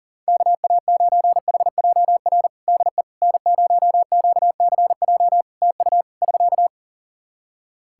Morse code, heard as a keyed tone, about 40 wpm, 700 Hz. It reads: KA9HJR DE N0QCJ TU <SK>
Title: Practice QSO - Exchange 4 - Ending - 40wpm